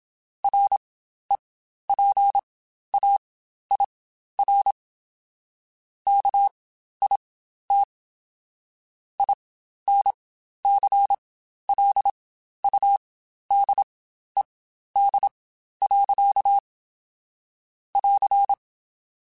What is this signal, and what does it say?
Morse code, heard as a keyed tone, about 27 wpm, spaced as for 12 wpm, 800 Hz. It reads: REPAIR KIT INCLUDED. <AR>
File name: Quotes Spaced / Quotes Spaced-mp3s / Airtight sp @25